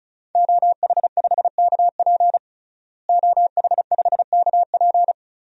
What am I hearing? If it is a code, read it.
Morse code, heard as a keyed tone, about 34 wpm, 700 Hz. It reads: OH5KP OH5KP
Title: DXCC-100 Call Signs - 5 Characters - Repeated 2x - 35wpm